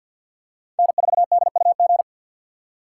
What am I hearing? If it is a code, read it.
Morse code, heard as a keyed tone, about 51 wpm, 700 Hz. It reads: N4DUG
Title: USA Call Signs - 5 Characters - 3x Character Spacing - 50wpm